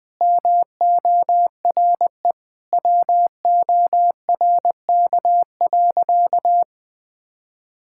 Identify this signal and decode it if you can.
Morse code, heard as a keyed tone, about 19 wpm, 700 Hz. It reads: MORE WORK.
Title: Sentences with 8-Word Phrases - Spoken Beginning - 20wpm